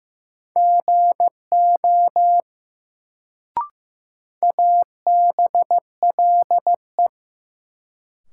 Morse code, heard as a keyed tone, about 15 wpm, 700 Hz. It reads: GO ABLE
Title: Sets of 2 Words - New Word Drill - 15wpm